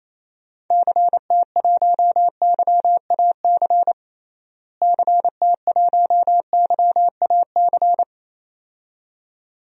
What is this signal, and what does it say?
Morse code, heard as a keyed tone, about 28 wpm, 700 Hz. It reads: CT1YAC CT1YAC
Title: DXCC-100 Call Signs - 6 Characters - Repeated 2x - 28wpm